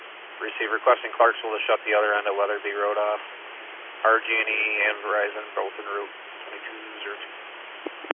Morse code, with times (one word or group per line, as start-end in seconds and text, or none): none